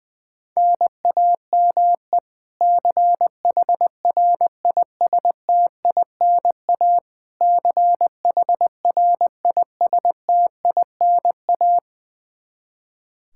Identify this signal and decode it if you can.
Morse code, heard as a keyed tone, about 20 wpm, 700 Hz. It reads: NAME CHRISTINA CHRISTINA
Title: K1USN SST QSO Practice 20wpm